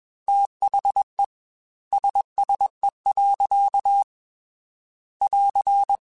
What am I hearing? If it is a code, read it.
Morse code, heard as a keyed tone, about 21 wpm, 800 Hz. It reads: THE SSE. <AR>
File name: Weather Sets / Weather Sets-mp3s / Weather 4 14 @20